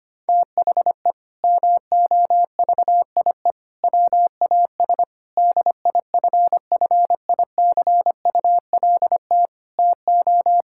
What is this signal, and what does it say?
Morse code, heard as a keyed tone, about 25 wpm, 700 Hz. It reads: THE MOVIE WAS DIFFICULT TO
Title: Sentences from Top 1000 Words - Review Part 5 - 25wpm